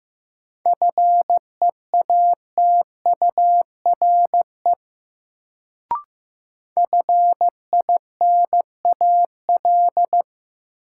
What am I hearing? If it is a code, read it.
Morse code, heard as a keyed tone, about 15 wpm, 700 Hz. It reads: FEATURE FINAL
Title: Sentences from Top 800 Words - New Word Drill - 15wpm